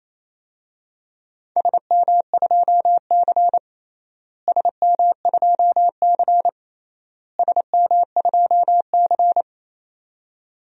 Morse code, heard as a keyed tone, about 28 wpm, 700 Hz. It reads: SM2C SM2C SM2C
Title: DXCC-100 Call Signs - 4 Characters - Repeated 3x - 28wpm